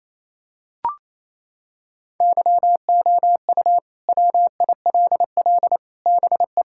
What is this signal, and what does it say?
Morse code, heard as a keyed tone, about 28 wpm, 700 Hz. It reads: YOU WILL BE
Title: Sentences from Top 1600 Words 28wpm